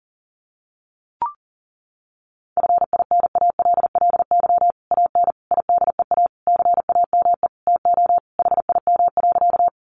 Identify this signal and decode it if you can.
Morse code, heard as a keyed tone, about 40 wpm, 700 Hz. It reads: FINALLY AN IDEA CAME TO HIM.
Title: Sentences from Top 1000 Words 40wpm